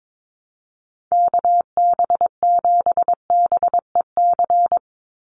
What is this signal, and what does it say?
Morse code, heard as a keyed tone, about 22 wpm, 700 Hz. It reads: KB7BEC
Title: USA Call Signs 22wpm